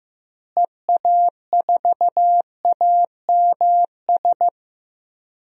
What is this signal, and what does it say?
Morse code, heard as a keyed tone, about 16 wpm, 700 Hz. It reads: EA4AMS
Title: DXCC-100 Call Signs - 6 Characters - Repeated 3x - 15wpm